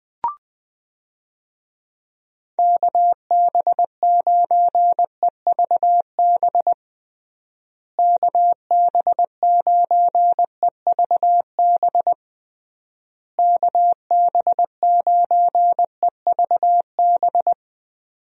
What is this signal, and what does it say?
Morse code, heard as a keyed tone, about 20 wpm, 700 Hz. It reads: KB9EVB KB9EVB KB9EVB
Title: USA Call Signs - 6 Characters - Repeated 3x - 20wpm